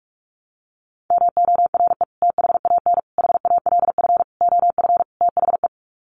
Morse code, heard as a keyed tone, about 44 wpm, 700 Hz. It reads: MORE THAN HALF OF THE
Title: Sets of 5 Words 45wpm